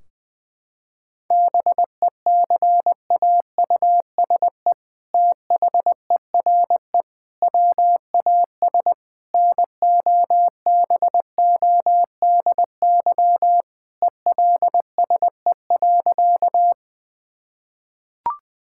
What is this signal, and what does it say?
Morse code, heard as a keyed tone, about 20 wpm, 700 Hz. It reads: BECAUSE THERE WAS NOBODY ELSE.
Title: Sentences from Top 1600 Words 20wpm